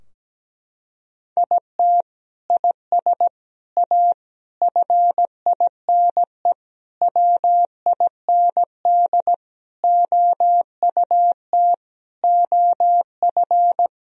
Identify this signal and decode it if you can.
Morse code, heard as a keyed tone, about 17 wpm, 700 Hz. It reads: IT IS A FINE WIND OUT OF
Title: Sentences from Top 1000 Words - Review Part 2 - 17wpm